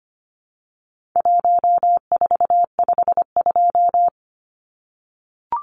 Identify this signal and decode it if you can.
Morse code, heard as a keyed tone, about 25 wpm, 700 Hz. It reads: 1452
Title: Numbers - Contest Style - 4-Digit 25wpm